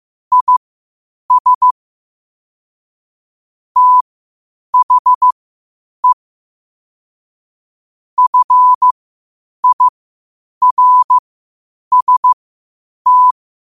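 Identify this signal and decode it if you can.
Morse code, heard as a keyed tone, about 15 wpm, spaced as for 8 wpm, 1000 Hz. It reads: IS THE FIRST
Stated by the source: Morse code at 8 wpm.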